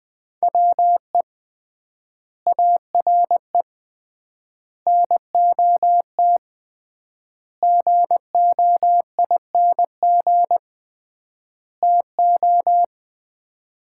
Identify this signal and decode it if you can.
Morse code, heard as a keyed tone, about 20 wpm, 700 Hz. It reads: WE ARE NOT GOING TO
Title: 3x Word-Spacing - Sets of 5 Words - 20wpm